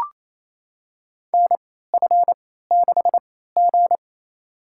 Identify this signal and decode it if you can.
Morse code, heard as a keyed tone, about 27 wpm, spaced as for 16 wpm, 700 Hz. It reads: NF6G
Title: USA Call Signs - 4 Characters - 3x Character Spacing - 28wpm